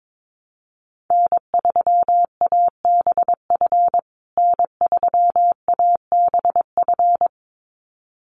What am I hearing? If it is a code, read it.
Morse code, heard as a keyed tone, about 22 wpm, 700 Hz. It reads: N3ABF N3ABF
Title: USA Call Signs Repeated 2x 22wpm